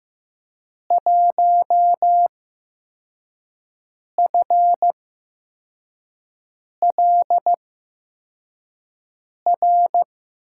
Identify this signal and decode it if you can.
Morse code, heard as a keyed tone, about 15 wpm, spaced as for 4 wpm, 700 Hz. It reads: 1FLR
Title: USA Call Signs - 5 Characters - 8x Character Spacing - 15wpm